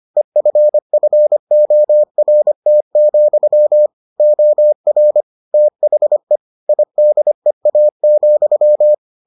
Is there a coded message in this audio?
Yes: Morse code from 0.2 to 8.9 s, heard as a keyed tone, about 25 wpm, 600 Hz. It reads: EFFORT, OR THE IDEA,